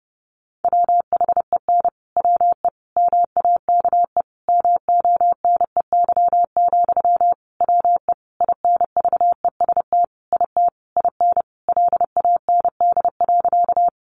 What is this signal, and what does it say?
Morse code, heard as a keyed tone, about 30 wpm, 700 Hz. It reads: WHEN WE MAKE MONEY, WE INVEST IT IN LAND.
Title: Sentences from Top 1100 Words 30wpm